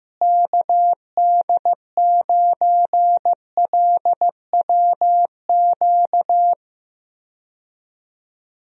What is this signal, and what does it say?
Morse code, heard as a keyed tone, about 15 wpm, 700 Hz. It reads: KD9LWQ
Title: USA Call Signs - 6 Characters - Repeated 3x - 15wpm